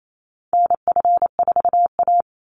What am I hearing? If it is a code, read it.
Morse code, heard as a keyed tone, about 28 wpm, 700 Hz. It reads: NF4A
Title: Call Signs - 28wpm